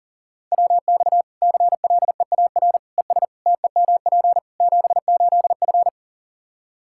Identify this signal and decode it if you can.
Morse code, heard as a keyed tone, about 40 wpm, 700 Hz. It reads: WX CLEAR ES TEMP 78F
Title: K1USN SST QSO Practice 40wpm